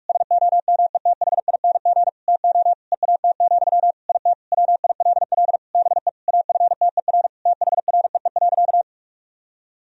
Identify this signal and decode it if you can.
Morse code, heard as a keyed tone, about 46 wpm, 700 Hz. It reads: SOMETHING TO EAT, IT WILL BE AFTER THREE.
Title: Sentences from Top 1000 Words - Review Part 1 - 45wpm